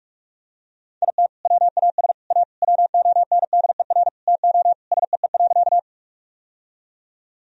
Morse code, heard as a keyed tone, about 41 wpm, 700 Hz. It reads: IT WAS A WONDER TO SEE.
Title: Sentences from Top 1000 Words - Review Part 3 - 45wpm